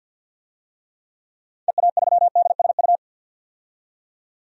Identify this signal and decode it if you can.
Morse code, heard as a keyed tone, about 51 wpm, 700 Hz. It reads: EA3DSU
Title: DXCC-100 Call Signs - 6 Characters - 50wpm